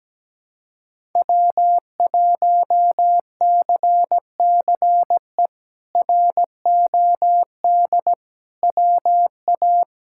Morse code, heard as a keyed tone, about 17 wpm, 700 Hz. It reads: W1CCE ROD WA
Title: K1USN SST QSO Practice - 17wpm